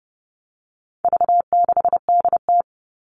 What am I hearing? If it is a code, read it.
Morse code, heard as a keyed tone, about 30 wpm, 700 Hz. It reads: V6DT
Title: All Cut Numbers - 4-Digit 30wpm